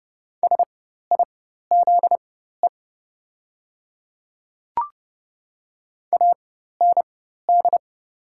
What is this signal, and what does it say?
Morse code, heard as a keyed tone, about 30 wpm, spaced as for 14 wpm, 700 Hz. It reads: SIZE AND
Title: 4x Character Spacing - Top 500 Words - 30wpm